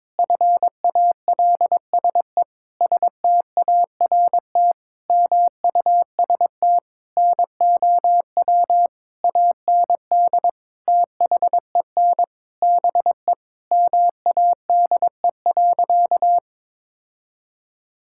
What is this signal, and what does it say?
Morse code, heard as a keyed tone, about 21 wpm, 700 Hz. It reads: FALSE START MUST NOW AND THEN BE MADE.
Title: Sentences from Top 1600 Words 22wpm